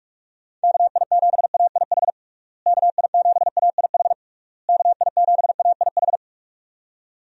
Morse code, heard as a keyed tone, about 46 wpm, 700 Hz. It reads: KI7AIH KI7AIH KI7AIH
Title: USA Call Signs - 6 Characters - Repeated 3x - 45wpm